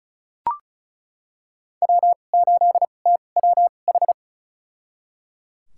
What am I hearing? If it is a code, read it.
Morse code, heard as a keyed tone, about 36 wpm, spaced as for 25 wpm, 700 Hz. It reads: W8TWH
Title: USA Call Signs - 5 Characters - 2x Character Spacing - 35wpm